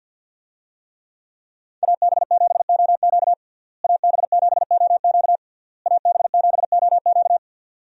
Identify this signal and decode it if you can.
Morse code, heard as a keyed tone, about 52 wpm, 700 Hz. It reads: AB7OQ AB7OQ AB7OQ
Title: USA Call Signs - 5 Characters - Repeated 3x - 50wpm